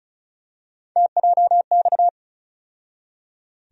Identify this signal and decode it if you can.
Morse code, heard as a keyed tone, about 36 wpm, 700 Hz. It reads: TJX
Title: USA Call Sign Postfixes - 3 Letters 35wpm